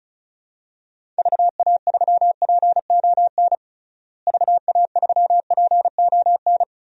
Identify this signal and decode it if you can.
Morse code, heard as a keyed tone, about 36 wpm, 700 Hz. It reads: VA3PON VA3PON
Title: DXCC-100 Call Signs - 6 Characters - Repeated 3x - 35wpm